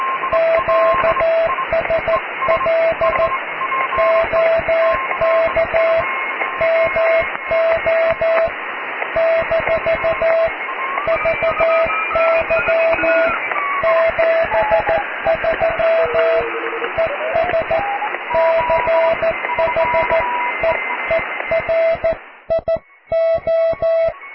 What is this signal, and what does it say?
Morse code, heard as a keyed tone, about 14 wpm, 660 Hz, with radio noise. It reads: QSL OK MO - VY 73 ES CHEERIO
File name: cw / QSO+QRM